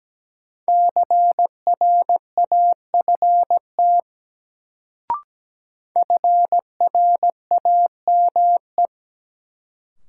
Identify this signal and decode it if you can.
Morse code, heard as a keyed tone, about 17 wpm, 700 Hz. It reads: CRAFT FRAME
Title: Sentences from Top 1000 Words - Review Part 6 - New Word Drill - 17wpm